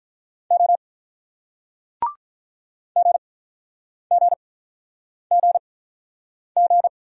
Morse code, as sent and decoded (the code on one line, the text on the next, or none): --- / --. --. --. --.
O GGGG